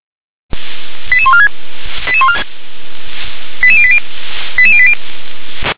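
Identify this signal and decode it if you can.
Five-tone selective call (ZVEI), heard as quick runs of five tones, tones of about 70 ms. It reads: 88146 88080 88080
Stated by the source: Received on an SDR.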